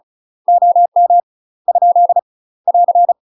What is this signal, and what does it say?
Morse code, heard as a keyed tone, about 36 wpm, 700 Hz. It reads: OM ? <AR>
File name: UK MASTER QSOs TRAINING/35WPM